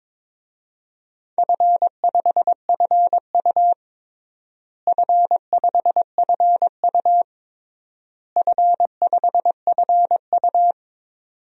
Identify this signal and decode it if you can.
Morse code, heard as a keyed tone, about 22 wpm, 700 Hz. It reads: F5FU F5FU F5FU
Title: DXCC-100 Call Signs - 4 Characters - Repeated 3x - 22wpm